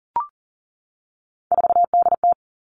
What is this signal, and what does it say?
Morse code, heard as a keyed tone, about 38 wpm, 700 Hz. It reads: 4DT
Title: Three Challenging Letter-Number Combinations 40wpm